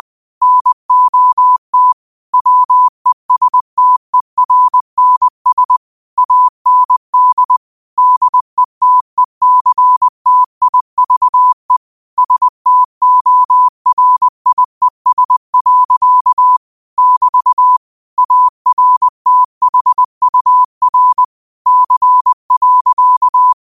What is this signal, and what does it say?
Morse code, heard as a keyed tone, about 20 wpm, 1000 Hz. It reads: NOT WESTERNS AND DETECTIVE STORIES. <BT> ARTHUR C.
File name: qotdfiles / 20wpm